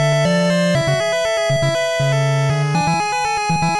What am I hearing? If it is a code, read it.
There is no code.